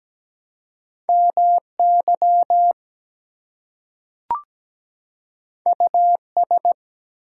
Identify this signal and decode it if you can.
Morse code, heard as a keyed tone, about 17 wpm, 700 Hz. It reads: MY US
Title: Two Letter Words - 17wpm